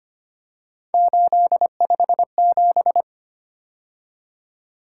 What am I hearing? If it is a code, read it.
Morse code, heard as a keyed tone, about 29 wpm, 700 Hz. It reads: 857
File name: Speed-Racing - Three Letter-Number Combinations - 25wpm